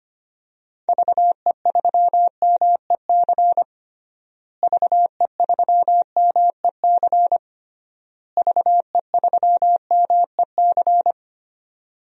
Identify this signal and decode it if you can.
Morse code, heard as a keyed tone, about 25 wpm, 700 Hz. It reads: VE3MEC VE3MEC VE3MEC
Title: DXCC-100 Call Signs - 6 Characters - Repeated 3x - 25wpm